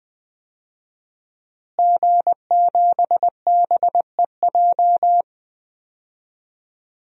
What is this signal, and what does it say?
Morse code, heard as a keyed tone, about 20 wpm, 700 Hz. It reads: G7BEJ
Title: DXCC-100 Call Signs - 5 Characters - Repeated 2x - 20wpm